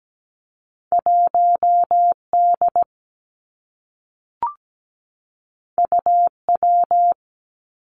Morse code, heard as a keyed tone, about 17 wpm, 700 Hz. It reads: 1D UW
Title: Two Challenging Letter-Number Combinations 17wpm